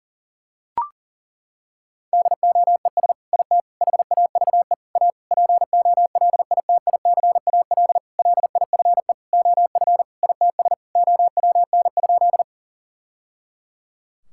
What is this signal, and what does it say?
Morse code, heard as a keyed tone, about 40 wpm, 690 Hz. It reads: DOES IT HAVE A POLITICAL LIFE OF ITS OWN?
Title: Sentences from Top 1000 Words - Review Part 5 - 40wpm